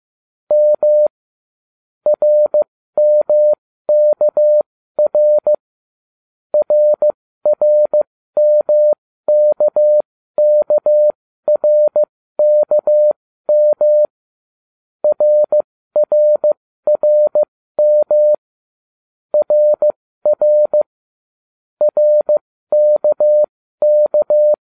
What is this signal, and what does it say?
Morse code, heard as a keyed tone, about 15 wpm, spaced as for 13 wpm, 600 Hz. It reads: M RMKR RRMKKRKM RRRM RR RKK